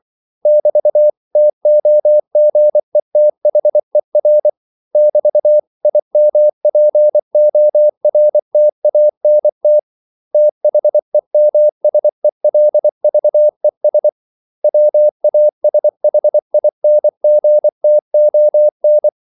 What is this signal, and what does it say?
Morse code, heard as a keyed tone, about 24 wpm, 600 Hz. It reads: <BT>TOGETHER <BT>IMPORTANT THEMSELVES WASHINGTON
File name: CW / Practice / 500MostCommonEnglishWords24